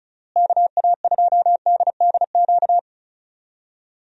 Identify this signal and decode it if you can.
Morse code, heard as a keyed tone, about 36 wpm, 700 Hz. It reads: KA2DDQ
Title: USA Call Signs - 6 Characters - Repeated 3x - 35wpm